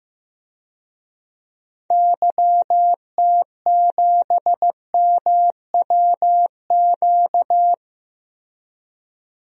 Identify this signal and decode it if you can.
Morse code, heard as a keyed tone, about 15 wpm, 700 Hz. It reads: YT7MWQ
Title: DXCC-100 Call Signs - 6 Characters - Repeated 3x - 15wpm